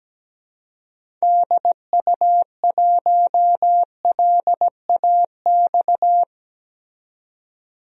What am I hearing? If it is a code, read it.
Morse code, heard as a keyed tone, about 17 wpm, 700 Hz. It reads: DU1LAX
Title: DXCC-100 Call Signs - 6 Characters - 17wpm